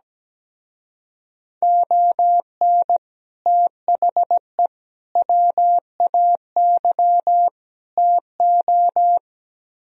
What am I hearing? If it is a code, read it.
Morse code, heard as a keyed tone, about 17 wpm, 700 Hz. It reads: ON THE WAY TO
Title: Speed-Racing - Sets of 4 Words 17wpm